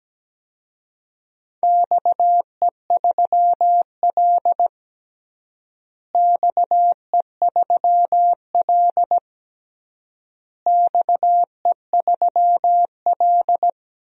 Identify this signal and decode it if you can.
Morse code, heard as a keyed tone, about 17 wpm, 700 Hz. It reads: XE3L XE3L XE3L
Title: DXCC-100 Call Signs - 4 Characters - Repeated 3x - 17wpm